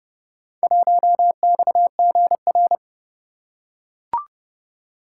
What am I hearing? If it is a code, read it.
Morse code, heard as a keyed tone, about 26 wpm, 700 Hz. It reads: 1XGR